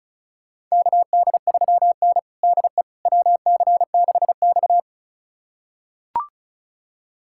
Morse code, heard as a keyed tone, about 36 wpm, 700 Hz. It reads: KD3N DE WC6X
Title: USA Callsign DE Callsign - 4 Characters - 35wpm